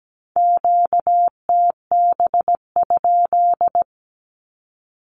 Morse code, heard as a keyed tone, about 17 wpm, 700 Hz. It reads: QTB?
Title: Q-Codes with Spoken Meaning - 17wpm